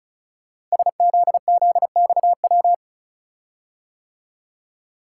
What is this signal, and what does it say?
Morse code, heard as a keyed tone, about 36 wpm, 700 Hz. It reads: SZZXW
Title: Code Groups - 5 Letters - 35wpm